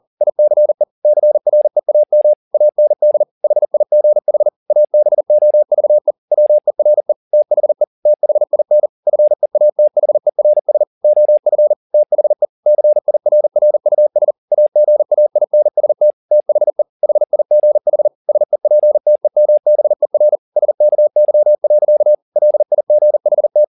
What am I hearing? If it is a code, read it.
Morse code, heard as a keyed tone, about 40 wpm, 600 Hz. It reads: ICE CREAM AND HIGH ABOVE WERE THE THIN FEATHERS OF THE CIRRUS AGAINST THE HIGH SEPTEMBER SKY. LIGHT